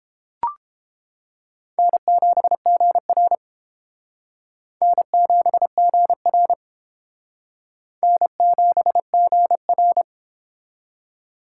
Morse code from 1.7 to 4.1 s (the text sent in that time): N7GR